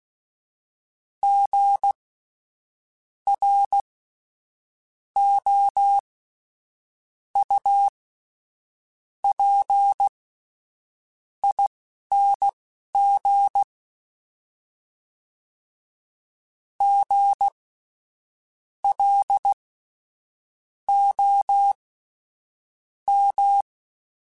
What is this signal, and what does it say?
Morse code, heard as a keyed tone, about 16 wpm, spaced as for 6 wpm, 800 Hz. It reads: GROUPING GLOM
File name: Spaced Alphas / Spaced Alphas mp3s / Gs sp @10